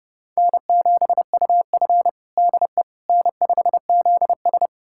Morse code, heard as a keyed tone, about 30 wpm, 700 Hz. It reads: N7UF DE N5ZS
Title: USA Callsign DE Callsign - 4 Characters - 30wpm